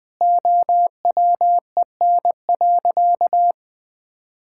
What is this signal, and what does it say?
Morse code, heard as a keyed tone, about 21 wpm, 700 Hz. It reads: OWEN.
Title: Pangram Challenge v2 - 20wpm